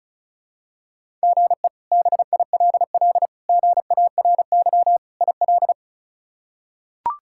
Morse code, heard as a keyed tone, about 34 wpm, 700 Hz. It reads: GE BILL GARY IL
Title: K1USN SST QSO Practice - 35wpm